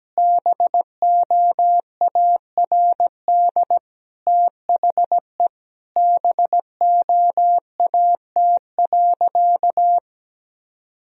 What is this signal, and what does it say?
Morse code, heard as a keyed tone, about 17 wpm, 700 Hz. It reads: BOARD THE BOAT.
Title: Sentences from Top 1000 Words - Review Part 4 - 17wpm